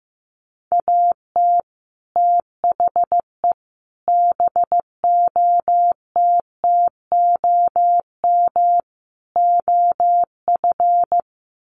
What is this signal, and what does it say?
Morse code, heard as a keyed tone, about 15 wpm, 700 Hz. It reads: AT THE BOTTOM OF
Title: Sets of 4 Words 15wpm